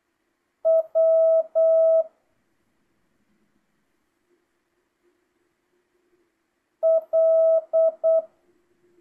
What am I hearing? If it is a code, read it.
Morse code, heard as a keyed tone, about 8 wpm, 650 Hz. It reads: W L